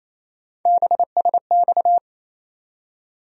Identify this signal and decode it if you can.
Morse code, heard as a keyed tone, about 28 wpm, 700 Hz. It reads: BSX
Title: USA Call Sign Postfixes - 3 Letters 28wpm